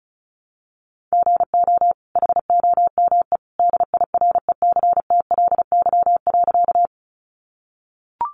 Morse code, heard as a keyed tone, about 34 wpm, 700 Hz. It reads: GO HOME DIRECTLY.
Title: Sentences from Top 800 Words 35wpm